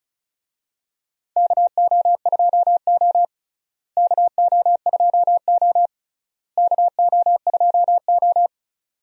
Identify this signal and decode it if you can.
Morse code, heard as a keyed tone, about 36 wpm, 700 Hz. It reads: KO2O KO2O KO2O
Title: USA Call Signs - 4 Characters - Repeated 3x - 35wpm